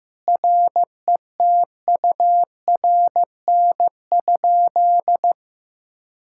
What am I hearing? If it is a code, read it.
Morse code, heard as a keyed tone, about 16 wpm, 700 Hz. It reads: RETURN?
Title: Sentences from Top 1600 Words 15wpm